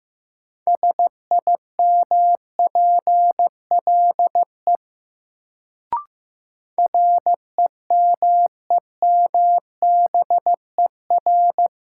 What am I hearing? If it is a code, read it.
Morse code, heard as a keyed tone, about 15 wpm, 700 Hz. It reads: SIMPLE REMEMBER
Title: Sentences from Top 400 Words - New Word Drill - 15wpm